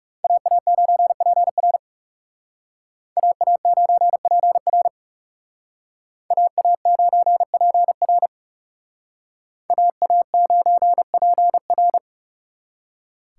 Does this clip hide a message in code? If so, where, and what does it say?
Yes: Morse code from 0.2 to 12.0 s, heard as a keyed tone, about 38 wpm, 700 Hz. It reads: AA9PR AA9PR AA9PR AA9PR